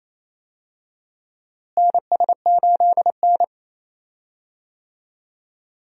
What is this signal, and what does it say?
Morse code, heard as a keyed tone, about 28 wpm, 700 Hz. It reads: NS8N
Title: USA Call Signs - 4 Characters - Repeated 3x - 28wpm